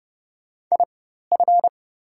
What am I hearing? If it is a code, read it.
Morse code, heard as a keyed tone, about 30 wpm, spaced as for 14 wpm, 700 Hz. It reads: IF
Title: 4x Character Spacing - Top 200 Words - 30wpm